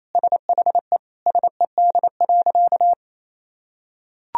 Morse code, heard as a keyed tone, about 28 wpm, 700 Hz. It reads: SHE SED.
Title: Sentences from Top 100 Words and Abbreviations 28wpm